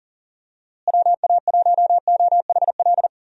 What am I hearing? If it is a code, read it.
Morse code, heard as a keyed tone, about 40 wpm, 700 Hz. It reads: WA1OHL
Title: USA Call Signs - 6 Characters - Repeated 2x - 40wpm